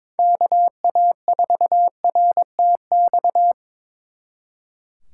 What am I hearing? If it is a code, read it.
Morse code, heard as a keyed tone, about 25 wpm, 700 Hz. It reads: KA4RTX
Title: Speed Racing - USA 2x3 Call Signs 22wpm